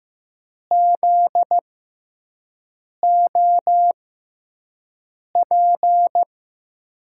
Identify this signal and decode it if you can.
Morse code, heard as a keyed tone, about 15 wpm, spaced as for 5 wpm, 700 Hz. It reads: ZOP
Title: USA Call Signs - 6x Character Spacing - 15wpm